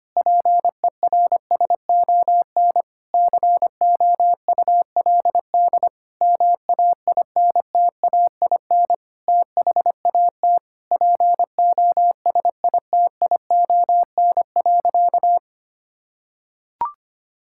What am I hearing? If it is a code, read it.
Morse code, heard as a keyed tone, about 25 wpm, 700 Hz. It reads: PERSON COULD MAINTAIN THAT POSITION.
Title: Sentences from Top 1000 Words - Review Part 5 - 25wpm